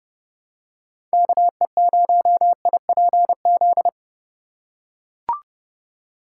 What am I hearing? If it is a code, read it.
Morse code, heard as a keyed tone, about 29 wpm, 700 Hz. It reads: KE0IPZ